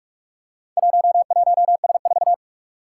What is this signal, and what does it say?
Morse code, heard as a keyed tone, about 44 wpm, 700 Hz. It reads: 11S4
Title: Four Challenging Letter-Number Combinations 45wpm